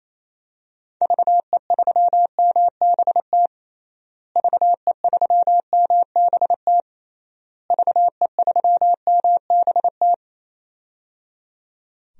Morse code, heard as a keyed tone, about 28 wpm, 700 Hz. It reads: VE3MBT VE3MBT VE3MBT
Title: DXCC-100 Call Signs - 6 Characters - Repeated 3x - 28wpm